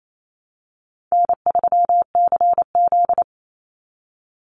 Morse code, heard as a keyed tone, about 28 wpm, 700 Hz. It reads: N3CZ
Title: Call Signs Repeated 2x - 28wpm